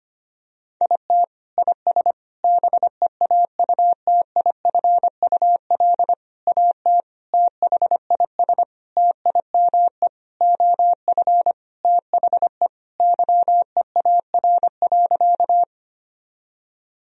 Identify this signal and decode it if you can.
Morse code, heard as a keyed tone, about 25 wpm, 700 Hz. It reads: IT IS BEAUTIFUL AT THIS TIME OF THE YEAR.
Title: Sentences from Top 1000 Words - Review Part 1 - 25wpm